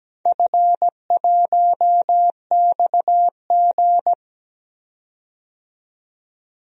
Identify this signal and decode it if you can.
Morse code, heard as a keyed tone, about 17 wpm, 700 Hz. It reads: F1XG
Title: DXCC-100 Call Signs - 4 Characters - 17wpm